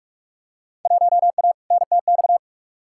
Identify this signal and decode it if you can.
Morse code, heard as a keyed tone, about 46 wpm, 700 Hz. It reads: 1A NTX
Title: Field Day Exchanges - 45wpm